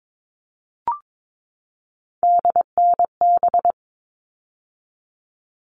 Morse code, heard as a keyed tone, about 22 wpm, 700 Hz. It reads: DNB
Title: All Cut Numbers - 3-Digit 22wpm